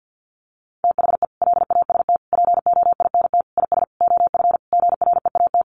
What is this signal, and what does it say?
Morse code, heard as a keyed tone, about 50 wpm, 700 Hz. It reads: THE LAST POINT IS OF GREAT
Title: Sentences from Top 1200 Words 50wpm